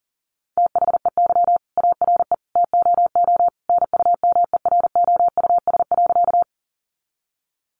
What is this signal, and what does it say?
Morse code, heard as a keyed tone, about 37 wpm, 700 Hz. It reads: THEY ARE TOO NUMEROUS.
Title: Sentences from Top 1400 Words 40wpm